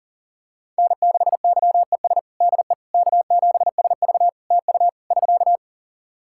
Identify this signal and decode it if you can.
Morse code, heard as a keyed tone, about 38 wpm, 700 Hz. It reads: N6YES DE K7SV TU <SK>
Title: K1USN SST QSO Practice 40wpm